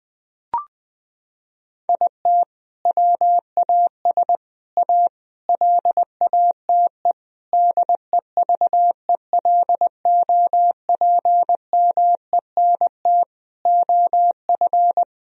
Morse code, heard as a keyed tone, about 21 wpm, 700 Hz. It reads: IT WAS A LATE DEVELOPMENT OF
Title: Sentences from Top 1000 Words - Review Part 4 - 20wpm